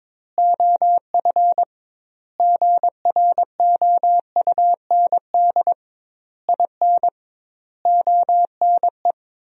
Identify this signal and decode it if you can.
Morse code, heard as a keyed tone, about 21 wpm, 700 Hz. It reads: OF GROUND IN ONE
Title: 2x Word-Spacing - Sentences from Top 500 Words - 22wpm